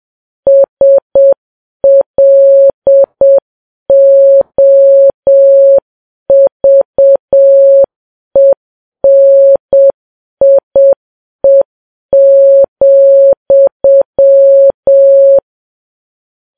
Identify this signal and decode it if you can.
Morse code, heard as a keyed tone, about 7 wpm, 550 Hz. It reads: SLOVENIE,